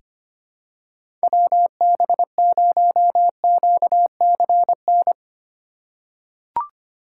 Morse code, heard as a keyed tone, about 25 wpm, 700 Hz. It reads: WB0QCN
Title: USA Call Signs - 6 Characters - 2x Character Spacing - 25wpm